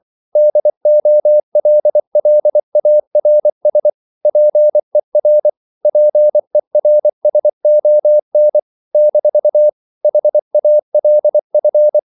Morse code, heard as a keyed tone, about 24 wpm, 600 Hz. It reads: DOLLARS PER PERSON - HALF